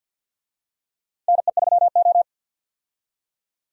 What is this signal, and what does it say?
Morse code, heard as a keyed tone, about 51 wpm, 700 Hz. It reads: NE3O
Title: USA Call Signs - 4 Characters - 50wpm